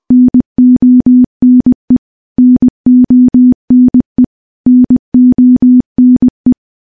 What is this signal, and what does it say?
Morse code, heard as a keyed tone, about 21 wpm, 250 Hz. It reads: NONE NONE NONE
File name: morse-training-20WPM